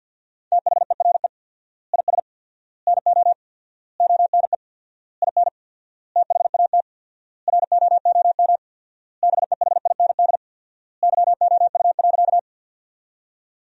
Morse code, heard as a keyed tone, about 51 wpm, 700 Hz. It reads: THERE IS NO ONE IN THAT ROOM BEHIND YOU.
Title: 4x Word-Spacing - Sentences from Top 500 Words - 50wpm